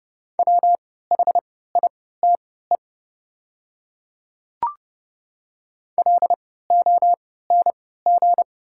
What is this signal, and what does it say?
Morse code, heard as a keyed tone, about 30 wpm, spaced as for 17 wpm, 700 Hz. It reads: WHITE LONG
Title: 3x Character Spacing - Top 300 Words - 30wpm